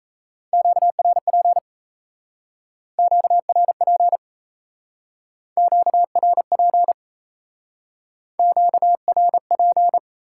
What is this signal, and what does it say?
Morse code, heard as a keyed tone, about 40 wpm, 700 Hz. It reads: QRP QRP QRP QRP